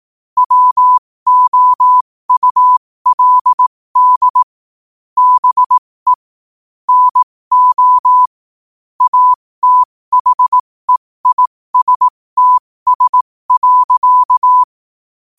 Morse code, heard as a keyed tone, about 18 wpm, spaced as for 15 wpm, 1000 Hz. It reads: WOULD BE NO ATHEISTS.
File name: qotdfiles / 15wpm